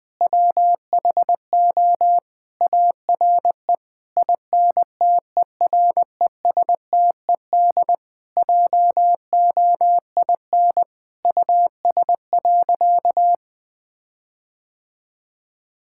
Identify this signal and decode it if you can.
Morse code, heard as a keyed tone, about 20 wpm, 700 Hz. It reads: WHO ARE INTERESTED JOIN US.
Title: Sentences with 6-Word Phrases - 20wpm